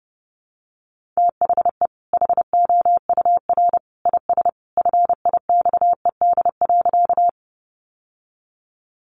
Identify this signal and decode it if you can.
Morse code, heard as a keyed tone, about 30 wpm, 700 Hz. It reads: THE HOUR IS FIXED.
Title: Sentences from Top 1100 Words 30wpm